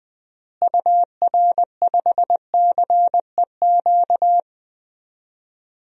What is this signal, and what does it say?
Morse code, heard as a keyed tone, about 20 wpm, 700 Hz. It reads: UR5CEQ
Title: DXCC-100 Call Signs - 6 Characters - Repeated 2x - 20wpm